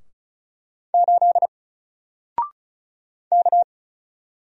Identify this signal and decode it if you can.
Morse code, heard as a keyed tone, about 34 wpm, 700 Hz. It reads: 8 K
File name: Single-Letter-Number-35wpm